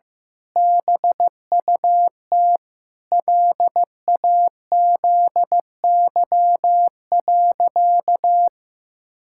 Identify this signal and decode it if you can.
Morse code, heard as a keyed tone, about 15 wpm, 700 Hz. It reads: BUT LAZY.
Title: Pangram Challenge v2 - 15wpm